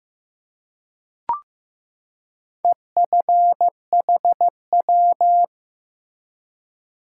Morse code, heard as a keyed tone, about 16 wpm, 700 Hz. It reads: EFHW
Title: Top 180 QSO Elements 15wpm